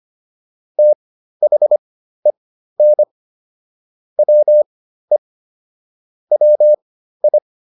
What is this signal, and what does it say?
Morse code, heard as a keyed tone, about 25 wpm, spaced as for 13 wpm, 600 Hz. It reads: THEN WE WI